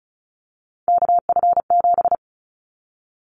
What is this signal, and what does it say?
Morse code, heard as a keyed tone, about 34 wpm, 700 Hz. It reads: KF7
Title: Callsign Prefixes for Top 25 DXCC - Repeated 3x - 35wpm